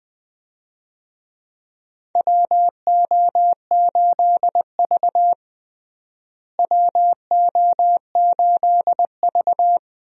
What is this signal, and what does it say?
Morse code, heard as a keyed tone, about 20 wpm, 700 Hz. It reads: WO8V WO8V
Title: USA Call Signs - 4 Characters - Repeated 2x - 20wpm